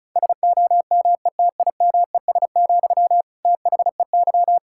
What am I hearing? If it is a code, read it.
Morse code, heard as a keyed tone, about 36 wpm, 700 Hz. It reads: SOMETIMES, THEY
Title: Sentences from Top 1000 Words - Review Part 6 - 35wpm